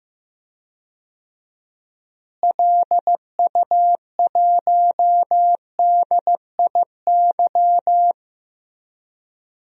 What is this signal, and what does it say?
Morse code, heard as a keyed tone, about 15 wpm, 700 Hz. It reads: LU1DIY
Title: DXCC-100 Call Signs - 6 Characters - Repeated 3x - 15wpm